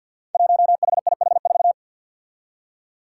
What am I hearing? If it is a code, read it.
Morse code, heard as a keyed tone, about 52 wpm, 700 Hz. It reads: 1HIH4
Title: Five Challenging Letter-Number Combinations 50wpm